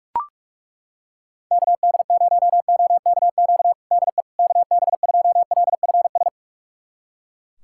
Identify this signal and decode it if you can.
Morse code, heard as a keyed tone, about 42 wpm, 700 Hz. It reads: KD0OKQ DE KB2LFS
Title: USA Callsign DE Callsign - 6 Characters - 45wpm